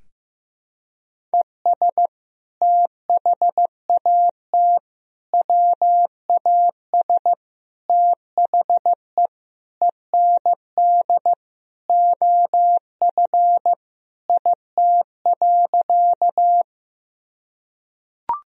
Morse code, heard as a keyed tone, about 15 wpm, 700 Hz. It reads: ES THAT WAS THE END OF IT.
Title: Sentences from Top 200 Words and Abbreviations 15wpm